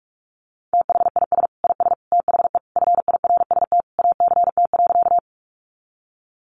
Morse code, heard as a keyed tone, about 44 wpm, 700 Hz. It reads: THIS IS THE FIRST ACT.
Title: Sentences from Top 200 Words 45wpm